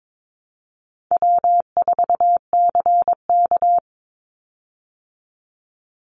Morse code, heard as a keyed tone, about 22 wpm, 700 Hz. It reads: W4CK
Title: Easy Call Signs Repeated 2x - 22wpm